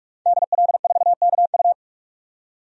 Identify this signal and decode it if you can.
Morse code, heard as a keyed tone, about 41 wpm, 700 Hz. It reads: DL4KU
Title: DXCC-100 Call Signs - 5 Characters - 45wpm